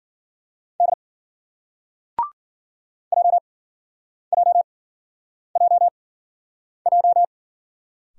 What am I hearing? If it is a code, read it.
Morse code, heard as a keyed tone, about 53 wpm, spaced as for 9 wpm, 700 Hz. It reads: D JJJJ